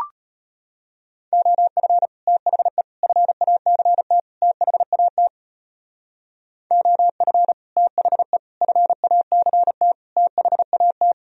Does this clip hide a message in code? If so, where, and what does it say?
Yes: Morse code from 1.3 to 11.1 s, heard as a keyed tone, about 33 wpm, 700 Hz. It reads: OF THE FACT THAT OF THE FACT THAT